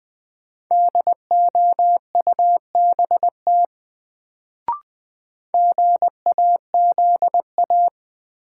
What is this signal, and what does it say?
Morse code, heard as a keyed tone, about 20 wpm, 700 Hz. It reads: DOUBT GAZA
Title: Sets of 5 Words - New Word Drill - 20wpm